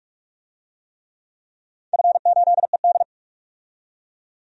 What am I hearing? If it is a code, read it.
Morse code, heard as a keyed tone, about 44 wpm, 700 Hz. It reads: F8ED
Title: DXCC-100 Call Signs - 4 Characters - 45wpm